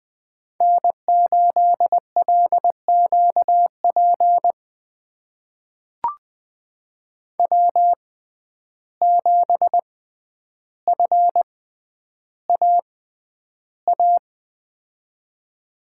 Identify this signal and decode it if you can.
Morse code, heard as a keyed tone, about 20 wpm, spaced as for 10 wpm, 700 Hz. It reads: N8LQP W7FAA
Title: USA Call Signs - 5 Characters - 6x Character Spacing - 20wpm